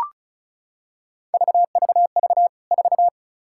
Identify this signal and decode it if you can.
Morse code, heard as a keyed tone, about 34 wpm, 700 Hz. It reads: VVV 4
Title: K1USN SST QSO Practice 35wpm